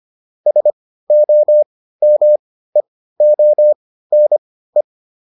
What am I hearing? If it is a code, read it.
Morse code, heard as a keyed tone, about 25 wpm, spaced as for 15 wpm, 600 Hz. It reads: SOMEONE